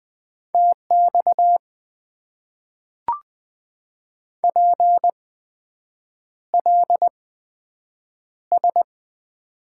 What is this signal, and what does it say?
Morse code, heard as a keyed tone, about 20 wpm, spaced as for 7 wpm, 700 Hz. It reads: TX PLS
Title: Top 90 QSO Elements - 8x Character Spacing - 20wpm